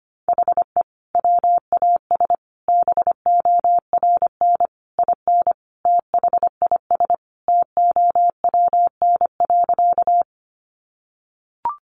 Morse code, heard as a keyed tone, about 25 wpm, 700 Hz. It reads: HE WAS BORN IN THIS TOWN.
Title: Sentences from Top 1000 Words 25wpm